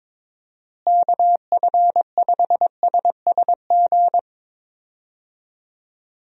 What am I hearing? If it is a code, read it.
Morse code, heard as a keyed tone, about 22 wpm, 700 Hz. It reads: KF5SSG
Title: USA Call Signs - 6 Characters - Repeated 3x - 22wpm